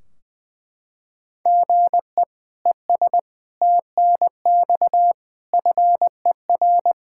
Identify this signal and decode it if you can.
Morse code, heard as a keyed tone, about 21 wpm, 700 Hz. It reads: GE ES TNX FER
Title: Practice QSO - Exchange 1 - Beginning - 20wpm